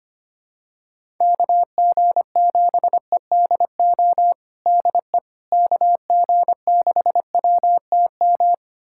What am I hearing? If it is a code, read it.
Morse code, heard as a keyed tone, about 25 wpm, 700 Hz. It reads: KG7EDO DE KG6WTM
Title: USA Callsign DE Callsign 25wpm